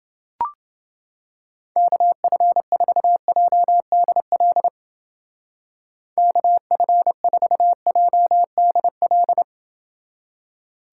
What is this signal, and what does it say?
Morse code, heard as a keyed tone, about 28 wpm, 700 Hz. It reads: KF4JDL KF4JDL